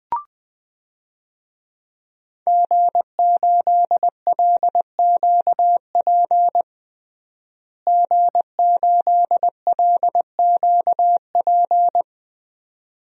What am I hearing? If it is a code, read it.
Morse code, heard as a keyed tone, about 21 wpm, 700 Hz. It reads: G8LQP G8LQP
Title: DXCC-100 Call Signs - 5 Characters - Repeated 2x - 20wpm